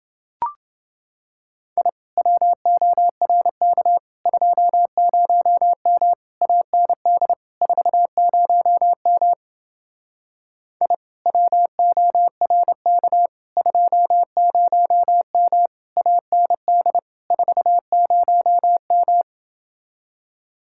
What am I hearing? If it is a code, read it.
Morse code, heard as a keyed tone, about 29 wpm, 700 Hz. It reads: I WORK 20M AND 40M I WORK 20M AND 40M